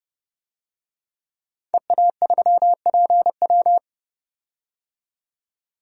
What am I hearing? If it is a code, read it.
Morse code, heard as a keyed tone, about 30 wpm, 700 Hz. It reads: EA3PW
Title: DXCC-100 Call Signs - 5 Characters - 30wpm